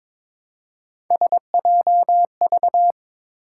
Morse code, heard as a keyed tone, about 21 wpm, 700 Hz. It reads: SJV
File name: ARRL Sections v2 - 22wpm